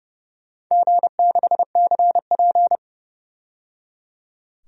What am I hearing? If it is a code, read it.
Morse code, heard as a keyed tone, about 31 wpm, 700 Hz. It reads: G6CP
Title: DXCC-100 Call Signs - 4 Characters - 30wpm